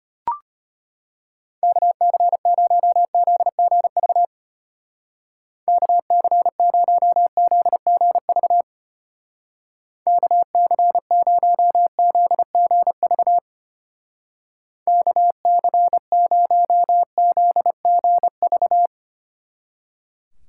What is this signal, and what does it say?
Morse code, heard as a keyed tone, about 31 wpm, 700 Hz. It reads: KC0ZGV KC0ZGV KC0ZGV KC0ZGV